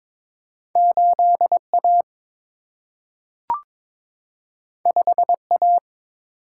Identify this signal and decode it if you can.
Morse code, heard as a keyed tone, about 22 wpm, 700 Hz. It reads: 8A 5A
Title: Single-Letter Single-Number 22wpm